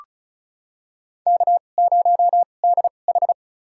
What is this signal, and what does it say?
Morse code, heard as a keyed tone, about 36 wpm, spaced as for 25 wpm, 700 Hz. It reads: K0DH
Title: USA Call Signs - 4 Characters - 2x Character Spacing - 35wpm